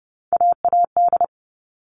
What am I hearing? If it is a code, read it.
Morse code, heard as a keyed tone, about 30 wpm, 700 Hz. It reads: AAD
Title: All Cut Numbers - 3-Digit 30wpm